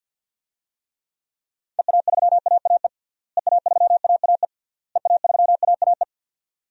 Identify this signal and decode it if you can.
Morse code, heard as a keyed tone, about 50 wpm, 700 Hz. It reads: EA3AAE EA3AAE EA3AAE
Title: DXCC-100 Call Signs - 6 Characters - Repeated 3x - 50wpm